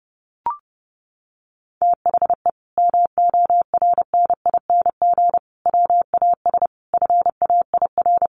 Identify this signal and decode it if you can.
Morse code, heard as a keyed tone, about 30 wpm, 700 Hz. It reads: THE MORNING WAS FAIR
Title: Sentences from Top 1300 Words 30wpm